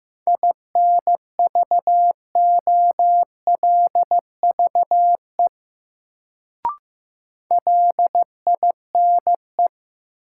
Morse code, heard as a keyed tone, about 15 wpm, 700 Hz. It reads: INVOLVE LINE
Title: Sentences from Top 1200 Words - New Word Drill Alt - 15wpm